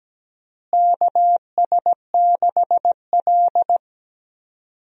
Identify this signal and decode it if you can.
Morse code, heard as a keyed tone, about 17 wpm, 700 Hz. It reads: KS6L
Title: USA Call Signs - 4 Characters - Repeated 2x - 17wpm